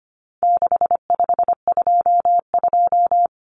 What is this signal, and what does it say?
Morse code, heard as a keyed tone, about 25 wpm, 700 Hz. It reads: 6522
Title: Common Cut Numbers - 4-Digit 25wpm